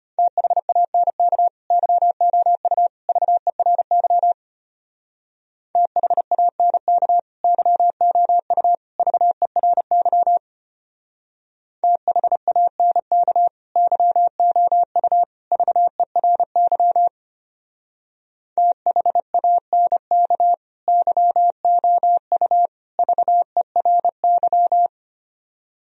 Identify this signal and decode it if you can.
Morse code, heard as a keyed tone, about 33 wpm, 700 Hz. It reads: THANK YOU VERY THANK YOU VERY THANK YOU VERY THANK YOU VERY